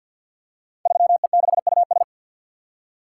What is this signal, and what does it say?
Morse code, heard as a keyed tone, about 48 wpm, 700 Hz. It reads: 3E6US
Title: Code Groups - 5 Letter:Numbers - 50wpm